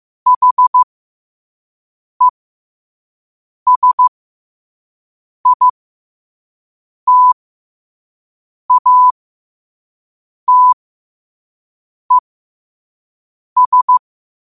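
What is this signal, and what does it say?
Morse code, heard as a keyed tone, about 15 wpm, spaced as for 5 wpm, 1000 Hz. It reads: HESITATES
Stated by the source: Morse code quote of the day at 5 wpm.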